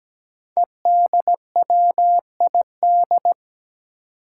Morse code, heard as a keyed tone, about 17 wpm, 700 Hz. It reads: EDWID
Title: Five Challenging Letter-Number Combinations 17wpm